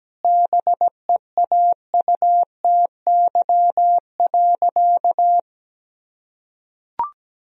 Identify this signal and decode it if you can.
Morse code, heard as a keyed tone, about 17 wpm, 700 Hz. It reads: BEAUTY.
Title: Sentences from Top 1000 Words - Review Part 2 - 17wpm